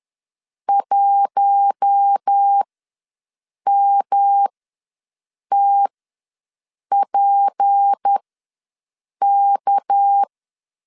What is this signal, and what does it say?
Morse code, heard as a keyed tone, about 11 wpm, spaced as for 6 wpm, 800 Hz. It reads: 1MTPK
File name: LEZIONI / 8 YJ45 / 8 YJ ws 10wpm 4043